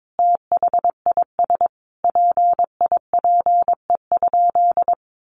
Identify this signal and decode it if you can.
Morse code, heard as a keyed tone, about 22 wpm, 700 Hz. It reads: THIS PIPE?
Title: Sentences from Top 1500 Words 22wpm